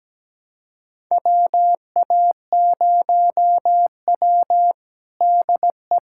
Morse code, heard as a keyed tone, about 17 wpm, 700 Hz. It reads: WA0W DE
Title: USA Callsign DE Callsign - 4 Characters - 17wpm